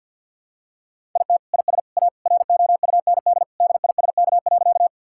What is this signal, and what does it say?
Morse code, heard as a keyed tone, about 52 wpm, 700 Hz. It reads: IT IS A ROUND DISK.
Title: Sentences from Top 1000 Words - Review Part 6 - 50wpm